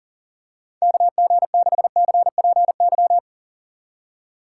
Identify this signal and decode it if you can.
Morse code, heard as a keyed tone, about 40 wpm, 700 Hz. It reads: KG6CPY
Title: USA Call Signs - 6 Characters - 3x Character Spacing - 40wpm